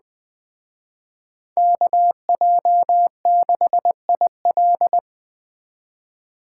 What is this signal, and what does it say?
Morse code, heard as a keyed tone, about 21 wpm, 700 Hz. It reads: KJ6IL